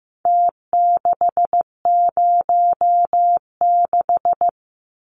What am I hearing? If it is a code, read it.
Morse code, heard as a keyed tone, about 15 wpm, 700 Hz. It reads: T606
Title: Numbers - Contest Style - 4-Digit 15wpm